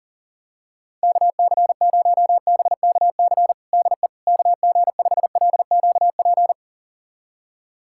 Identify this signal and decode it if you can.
Morse code, heard as a keyed tone, about 40 wpm, 700 Hz. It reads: KC0BKC DE KG5LQP
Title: USA Callsign DE Callsign - 6 Characters - 40wpm